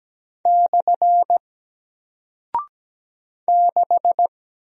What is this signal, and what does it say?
Morse code, heard as a keyed tone, about 17 wpm, 700 Hz. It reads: / 6
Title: Single Character (Letter:Number:Punctuation:Prosign) Alt 17wpm